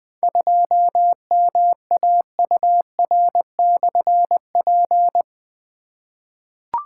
Morse code, heard as a keyed tone, about 20 wpm, 700 Hz. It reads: 2MAUR/P
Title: Difficult Call Signs 20wpm